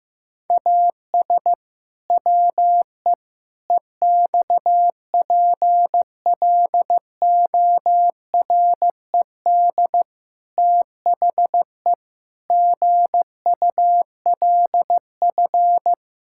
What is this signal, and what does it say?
Morse code, heard as a keyed tone, about 15 wpm, 700 Hz. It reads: AS WE EXPLORED THE GULF
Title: Pangram Challenge v2 - 15wpm